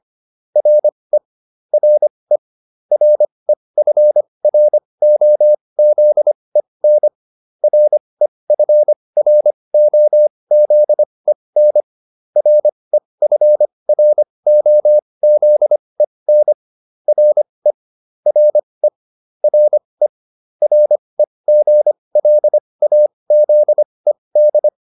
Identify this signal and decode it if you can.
Morse code, heard as a keyed tone, about 25 wpm, spaced as for 20 wpm, 600 Hz. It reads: RE RE REFROZEN REFROZEN REFROZEN RE RE RE REGLAZED